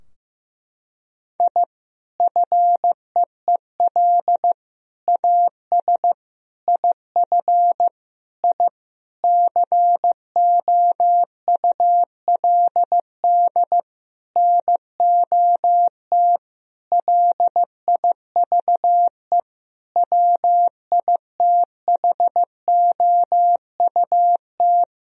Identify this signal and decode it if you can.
Morse code, heard as a keyed tone, about 15 wpm, 700 Hz. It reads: I FEEL AS IF I COULD NOT LIVE WITHOUT
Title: Sentences from Top 1000 Words - Review Part 3 - 15wpm